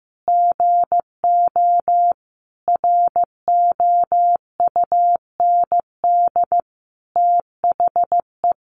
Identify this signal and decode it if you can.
Morse code, heard as a keyed tone, about 15 wpm, 700 Hz. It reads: GO ROUND THE
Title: Sentences from Top 600 Words 15wpm